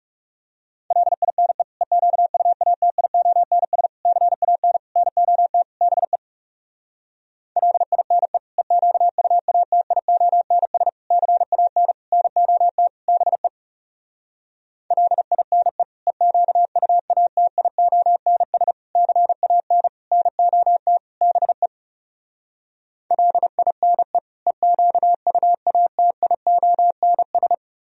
Morse code, heard as a keyed tone, about 39 wpm, 700 Hz. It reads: LINE EQUATIONS CAN NOT BE LINE EQUATIONS CAN NOT BE LINE EQUATIONS CAN NOT BE LINE EQUATIONS